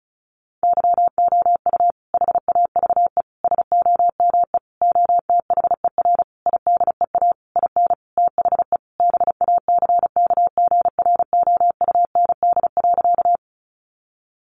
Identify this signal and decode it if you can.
Morse code, heard as a keyed tone, about 34 wpm, 700 Hz. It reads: YOU HAVE SOME OTHER IDEA IN THE BACKGROUND.
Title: Sentences from Top 1300 Words 35wpm